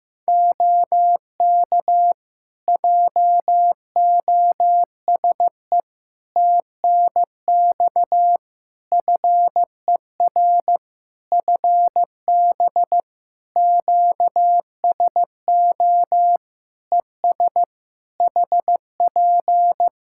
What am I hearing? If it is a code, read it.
Morse code, heard as a keyed tone, about 15 wpm, 700 Hz. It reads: OK JOSE TNX FER FB QSO ES HP
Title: K1USN SST QSO Practice 15wpm